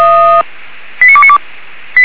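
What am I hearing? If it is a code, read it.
Five-tone selective call (ZVEI), heard as quick runs of five tones, tones of about 70 ms. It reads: 89292
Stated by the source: Five-tone ZVEI selective call 89297 received by SDR, logged by fire brigade Feuerwehr Orlamünde.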